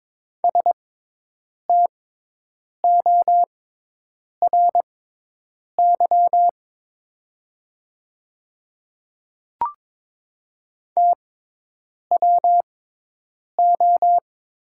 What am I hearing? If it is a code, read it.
Morse code, heard as a keyed tone, about 22 wpm, spaced as for 8 wpm, 700 Hz. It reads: STORY TWO
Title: 6x Character Spacing - Top 300 Words - 22wpm